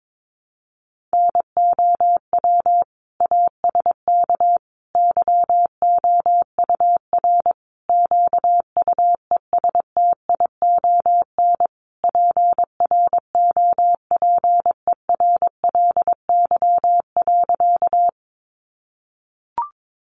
Morse code, heard as a keyed tone, about 22 wpm, 700 Hz. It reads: NOW ASK YOUR QUESTION PROPERLY.
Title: Sentences from Top 1100 Words 22wpm